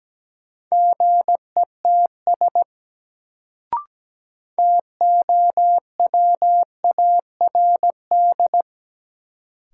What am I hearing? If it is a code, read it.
Morse code, heard as a keyed tone, about 17 wpm, 700 Hz. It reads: GETS TOWARD
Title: Sentences from Top 1300 Words - New Word Drill Alt - 17wpm